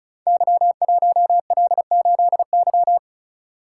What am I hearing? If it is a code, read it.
Morse code, heard as a keyed tone, about 36 wpm, 700 Hz. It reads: Y1L8Y
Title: Code Groups - 5 Letter:Numbers - 35wpm